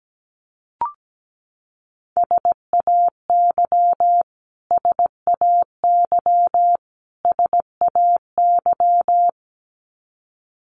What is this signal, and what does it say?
Morse code, heard as a keyed tone, about 17 wpm, 700 Hz. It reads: SAY SAY SAY
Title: Top 500 Words Repeated 17wpm